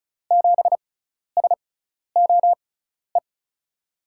Morse code, heard as a keyed tone, about 36 wpm, spaced as for 12 wpm, 700 Hz. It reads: 7SOE
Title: USA Call Signs - 6 Characters - 6x Character Spacing - 35wpm